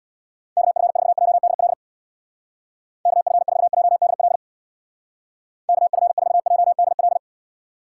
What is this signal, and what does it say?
Morse code, heard as a keyed tone, about 64 wpm, 700 Hz. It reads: KF4JDL KF4JDL KF4JDL